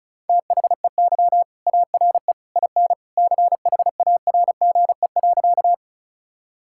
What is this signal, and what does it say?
Morse code, heard as a keyed tone, about 34 wpm, 700 Hz. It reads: THEY ARE IN CHARGE.
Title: Sentences from Top 1000 Words - Review Part 6 - 35wpm